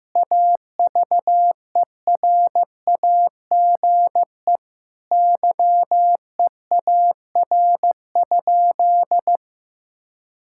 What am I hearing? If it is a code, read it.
Morse code, heard as a keyed tone, about 15 wpm, 700 Hz. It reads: AVERAGE YEAR?
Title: Sentences from Top 1000 Words - Review Part 6 - 15wpm